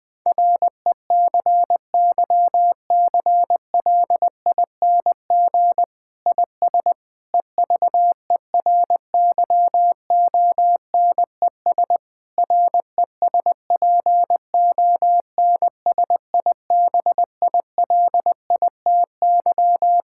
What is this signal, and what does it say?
Morse code, heard as a keyed tone, about 20 wpm, 700 Hz. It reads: RECYCLING IS EVERYONES RESPONSIBILITY
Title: Difficult Sentences with Hints - 20wpm